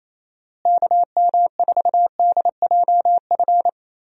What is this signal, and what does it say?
Morse code, heard as a keyed tone, about 28 wpm, 700 Hz. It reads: KM4DJF
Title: USA Call Signs - 6 Characters - Repeated 2x - 28wpm